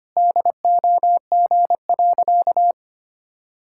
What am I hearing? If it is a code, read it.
Morse code, heard as a keyed tone, about 25 wpm, 700 Hz. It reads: DOG.
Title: Pangram Challenge v2 - 25wpm